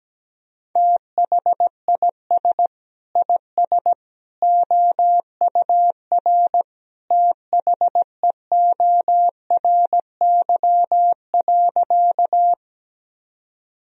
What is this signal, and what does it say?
Morse code, heard as a keyed tone, about 17 wpm, 700 Hz. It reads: THIS IS OUR THEORY.
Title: Sentences from Top 1000 Words - Review Part 3 - 17wpm